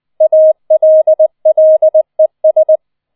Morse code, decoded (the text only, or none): ALLES